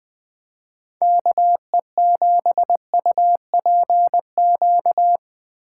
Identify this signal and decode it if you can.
Morse code, heard as a keyed tone, about 19 wpm, 700 Hz. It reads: KE7UPQ
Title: Speed Racing - USA 2x3 Call Signs 17wpm